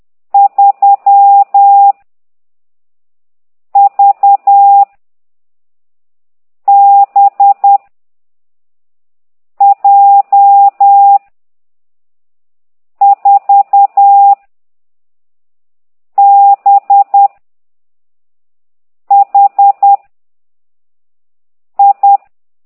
Morse code, heard as a keyed tone, about 10 wpm, 800 Hz. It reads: 3 V B J 4 B H I
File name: ct2idw / exerc15